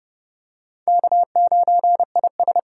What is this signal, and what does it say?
Morse code, heard as a keyed tone, about 28 wpm, 700 Hz. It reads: K9IS